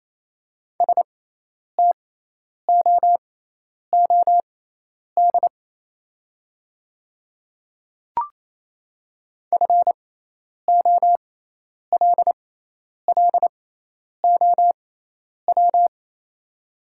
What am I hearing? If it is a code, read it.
Morse code, heard as a keyed tone, about 28 wpm, spaced as for 10 wpm, 690 Hz. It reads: STOOD FOLLOW
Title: 6x Character Spacing - Top 500 Words - 28wpm